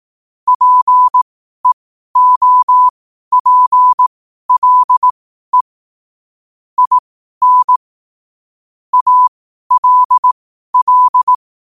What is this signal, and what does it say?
Morse code, heard as a keyed tone, about 18 wpm, spaced as for 12 wpm, 1000 Hz. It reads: PEOPLE IN ALL
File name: qotdfiles / 12wpm